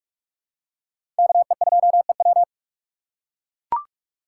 Morse code, heard as a keyed tone, about 44 wpm, 700 Hz. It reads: KE2EW
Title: USA Call Signs - 5 Characters - 6x Character Spacing - 45wpm